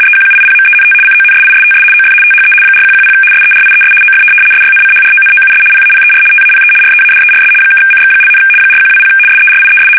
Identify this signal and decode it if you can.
RTTY, about 75 baud. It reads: EQUIPMENT CONSISTS OF THE 4110 SERIES OF SYNCHRONOUS PAPER TAPE READERS (CX), 4120 SERIES
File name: fsk850-75B / teletype-corp